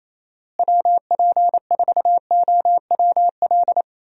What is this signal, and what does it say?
Morse code, heard as a keyed tone, about 27 wpm, 700 Hz. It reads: WP4OWL
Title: USA Call Signs - 6 Characters - Repeated 2x - 28wpm